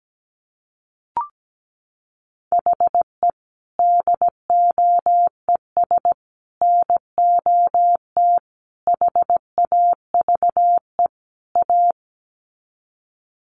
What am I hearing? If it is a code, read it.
Morse code, heard as a keyed tone, about 17 wpm, 700 Hz. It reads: HE DOES NOT HAVE A
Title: Sets of 5 Words Encore 17wpm